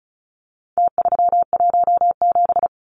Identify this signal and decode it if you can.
Morse code, heard as a keyed tone, about 34 wpm, 700 Hz. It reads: T317
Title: Numbers - Contest Style - 4-Digit 35wpm